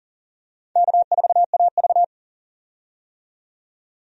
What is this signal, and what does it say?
Morse code, heard as a keyed tone, about 40 wpm, 700 Hz. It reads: K4AV
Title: USA Call Signs - 4 Characters - 40wpm